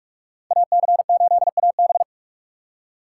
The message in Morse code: AC8AB

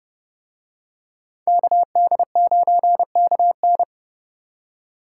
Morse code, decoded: KD9KN

